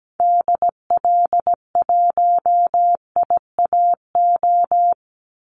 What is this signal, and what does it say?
Morse code, heard as a keyed tone, about 17 wpm, 700 Hz. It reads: DL1IAO
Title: Call Signs Repeated 2x - 17wpm